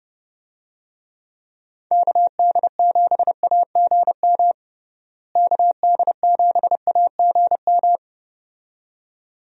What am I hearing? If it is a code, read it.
Morse code, heard as a keyed tone, about 30 wpm, 700 Hz. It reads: KD7AGM KD7AGM
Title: USA Call Signs - 6 Characters - Repeated 2x - 30wpm